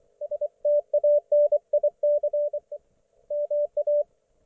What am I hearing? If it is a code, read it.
Morse code, heard as a keyed tone, about 24 wpm, 590 Hz. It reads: STANIC MA
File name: Archiv / CUC212 / ok1fhi cul212